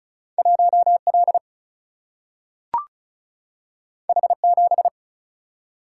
Morse code, heard as a keyed tone, about 36 wpm, 700 Hz. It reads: 1 L H 7
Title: Single-Letter Single-Number 35wpm